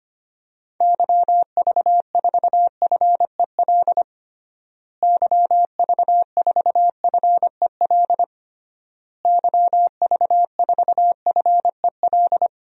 Morse code, heard as a keyed tone, about 25 wpm, 700 Hz. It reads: YV4FEL YV4FEL YV4FEL
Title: DXCC-100 Call Signs - 6 Characters - Repeated 3x - 25wpm